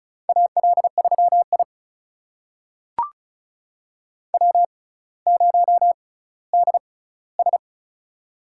Morse code, heard as a keyed tone, about 34 wpm, spaced as for 18 wpm, 700 Hz. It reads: AL3I W0DS
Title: USA Call Signs - 6x Character Spacing - 35wpm